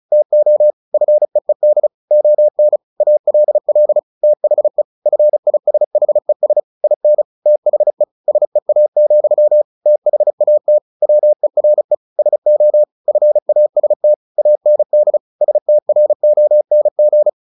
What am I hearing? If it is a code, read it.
Morse code, heard as a keyed tone, about 34 wpm, 600 Hz. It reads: TO FEED ON ALL THE FISHES IN THE SEA, THAT WERE SO FAST AND STRONG
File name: the old man and the sea 2 35wpm